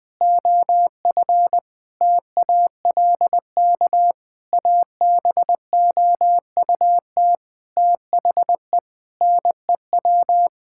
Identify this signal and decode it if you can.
Morse code, heard as a keyed tone, about 20 wpm, 700 Hz. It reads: OF TALK ABOUT THE NEW
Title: Sentences with 8-Word Phrases - 20wpm